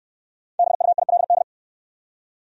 Morse code, heard as a keyed tone, about 61 wpm, 700 Hz. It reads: BUILD